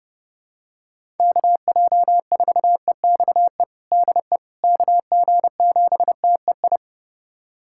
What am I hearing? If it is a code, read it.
Morse code, heard as a keyed tone, about 32 wpm, 700 Hz. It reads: KJ4EXE DE KG7TEI
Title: USA Callsign DE Callsign - 6 Characters - 30wpm